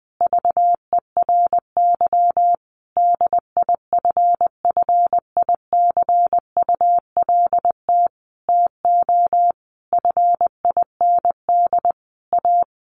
Morse code, heard as a keyed tone, about 20 wpm, 700 Hz. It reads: VERY DIFFICULT TO FIND A
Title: Sentences from Top 1100 Words 20wpm